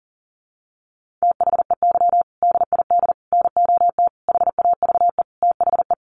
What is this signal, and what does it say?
Morse code, heard as a keyed tone, about 40 wpm, 700 Hz. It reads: THEY DID NOT HAVE THE
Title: Sentences from Top 1500 Words 40wpm